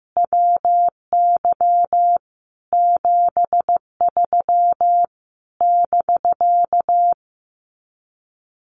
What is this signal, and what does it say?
Morse code, heard as a keyed tone, about 15 wpm, 700 Hz. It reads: WY 73 <BK>
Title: Common QSO Phrases Encore 15wpm